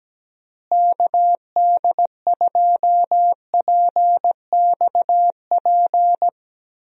Morse code, heard as a keyed tone, about 17 wpm, 700 Hz. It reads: KD2PXP
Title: USA Call Signs - 6 Characters - Repeated 3x - 17wpm